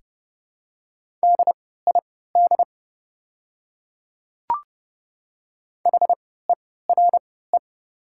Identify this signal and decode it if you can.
Morse code, heard as a keyed tone, about 30 wpm, spaced as for 17 wpm, 700 Hz. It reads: DID HERE
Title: 3x Character Spacing - Top 300 Words - 30wpm